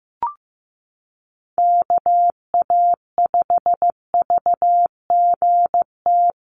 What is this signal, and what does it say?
Morse code, heard as a keyed tone, about 15 wpm, 700 Hz. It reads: KA5VGT
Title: USA Call Signs Repeated 3x 15wpm